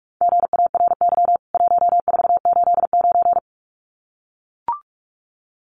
Morse code, heard as a keyed tone, about 44 wpm, 700 Hz. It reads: GARY 1489
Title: CWT Exchange 45wpm